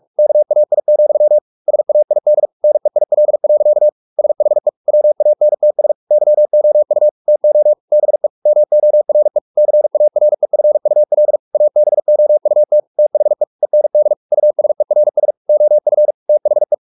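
Morse code, heard as a keyed tone, about 44 wpm, 600 Hz. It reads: KAI, SAID NEIL. SHE WANTS YOU TO BE MORE CAREFUL ABOUT THE END USERS OF THE
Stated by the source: Morse practice at 45 wpm, the novel Never by Ken Follett in English, from CW med Gnister.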